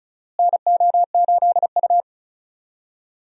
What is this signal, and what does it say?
Morse code, heard as a keyed tone, about 36 wpm, 700 Hz. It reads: NO8U
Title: USA Call Signs - 4 Characters - Repeated 3x - 35wpm